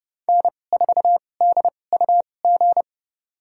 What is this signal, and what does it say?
Morse code, heard as a keyed tone, about 30 wpm, spaced as for 22 wpm, 700 Hz. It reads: N4DUG
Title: USA Call Signs - 2x Character Spacing - 30wpm